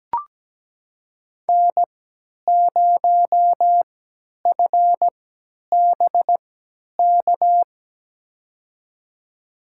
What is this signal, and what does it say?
Morse code, heard as a keyed tone, about 17 wpm, spaced as for 10 wpm, 700 Hz. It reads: N0FBK
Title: USA Call Signs - 3x Character Spacing - 17wpm